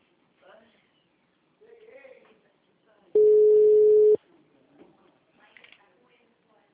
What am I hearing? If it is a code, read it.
Ringback tone (Europe, 425 Hz), with speech.